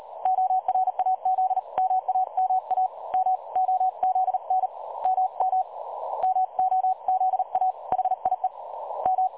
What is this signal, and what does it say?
Morse code, heard as a keyed tone, about 38 wpm, 750 Hz. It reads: ORAZ MAMA M OZN MM MOZA HIE M
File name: dlazaawansowanych / nagraniazpasma / websdr recording 3565.0kHz